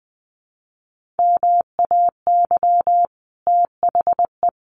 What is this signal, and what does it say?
Morse code, heard as a keyed tone, about 20 wpm, 700 Hz. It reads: MAY THE
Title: Sentences from Top 1500 Words 20wpm